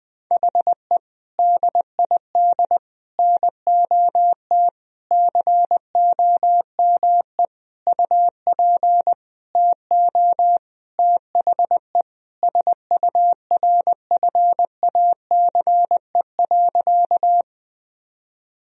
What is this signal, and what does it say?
Morse code, heard as a keyed tone, about 20 wpm, 700 Hz. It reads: HE DID NOT COME UP TO THE SURFACE.
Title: Sentences from Top 1000 Words - Review Part 3 - 20wpm